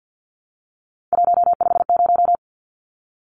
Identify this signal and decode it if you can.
Morse code, heard as a keyed tone, about 52 wpm, 700 Hz. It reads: 150
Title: Numbers - 3-Digit 50wpm